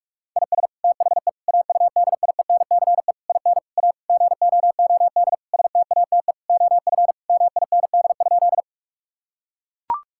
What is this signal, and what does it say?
Morse code, heard as a keyed tone, about 42 wpm, 700 Hz. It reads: IS THE AUDIENCE IN A GOOD STATE OF MIND?
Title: Sentences from Top 1000 Words - Review Part 5 - 45wpm